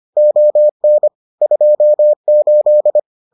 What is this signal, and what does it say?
Morse code, heard as a keyed tone, about 25 wpm, 600 Hz. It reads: ON 28